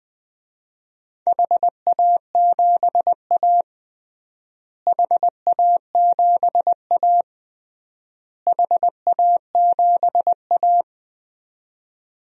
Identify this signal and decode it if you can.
Morse code, heard as a keyed tone, about 21 wpm, 700 Hz. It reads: HA7A HA7A HA7A
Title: DXCC-100 Call Signs - 4 Characters - Repeated 3x - 20wpm